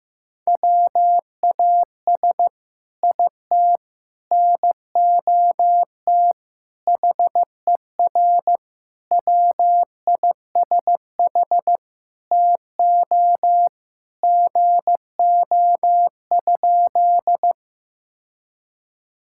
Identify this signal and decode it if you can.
Morse code, heard as a keyed tone, about 15 wpm, 700 Hz. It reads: WAS IT NOT HER WISH TO GO?
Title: Sentences from Top 1600 Words 15wpm